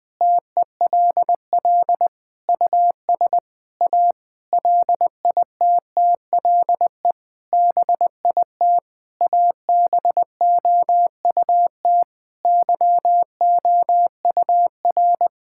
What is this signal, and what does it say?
Morse code, heard as a keyed tone, about 20 wpm, 700 Hz. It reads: TELL US A LITTLE BIT ABOUT YOUR
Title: Sentences with 8-Word Phrases - Spoken Beginning - 20wpm